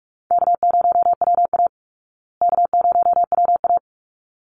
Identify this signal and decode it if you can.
Morse code, heard as a keyed tone, about 44 wpm, 700 Hz. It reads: K0WA K0WA
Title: Easy Call Signs Repeated 2x - 45wpm